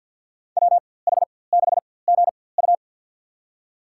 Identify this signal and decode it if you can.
Morse code, heard as a keyed tone, about 53 wpm, spaced as for 24 wpm, 700 Hz. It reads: WH6GU
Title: USA Call Signs - 5 Characters - 4x Character Spacing - 50wpm